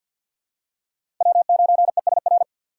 Morse code, heard as a keyed tone, about 47 wpm, 700 Hz. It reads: W9ESR
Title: USA Call Signs - 5 Characters - 4x Character Spacing - 50wpm